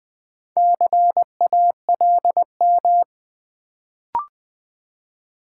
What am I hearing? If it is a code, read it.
Morse code, heard as a keyed tone, about 20 wpm, 700 Hz. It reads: CALM
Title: Build A Word - 1 Letter At A Time - 4-Letter Words - 20wpm